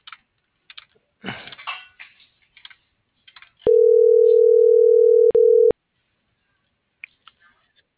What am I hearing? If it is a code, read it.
Ringback tone (North America), 440 + 480 Hz.
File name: CENTRAL MABE / out xquenda.trinidad - .74725